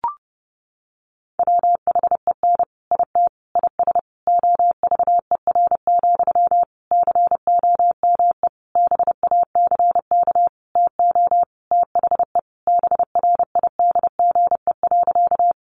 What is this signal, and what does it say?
Morse code, heard as a keyed tone, about 30 wpm, 700 Hz. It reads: WHEN IT IS OVER, COME BACK TO THE BRIDGE.
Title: Sentences from Top 1200 Words 30wpm